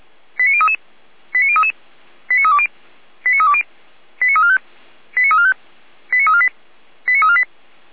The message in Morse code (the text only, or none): none